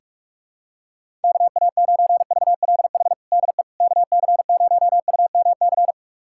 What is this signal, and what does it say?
Morse code, heard as a keyed tone, about 41 wpm, 700 Hz. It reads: KA9VLH DE KC0UMC
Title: USA Callsign DE Callsign - 6 Characters - 45wpm